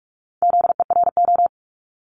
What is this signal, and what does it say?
Morse code, heard as a keyed tone, about 46 wpm, 690 Hz. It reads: ZERO
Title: Multi-letter Phonemes 45wpm